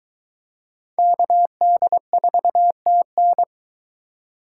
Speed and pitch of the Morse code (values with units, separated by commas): 23 wpm, 700 Hz